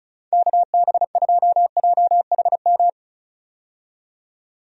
Morse code, heard as a keyed tone, about 34 wpm, 700 Hz. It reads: KB2JHM
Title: USA Call Signs - 6 Characters - 35wpm